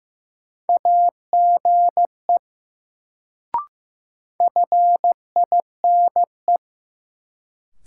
Morse code, heard as a keyed tone, about 15 wpm, 700 Hz. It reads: AGE FINE
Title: Sentences from Top 500 Words - New Word Drill - 15wpm